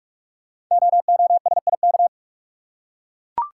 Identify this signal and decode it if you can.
Morse code, heard as a keyed tone, about 44 wpm, 700 Hz. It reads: OOSIK
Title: Code Groups - 5 Letters - 45wpm